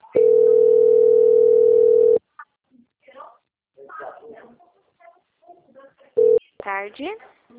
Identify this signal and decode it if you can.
Ringback tone (North America), 440 + 480 Hz, with speech.
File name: out- - - - - .18968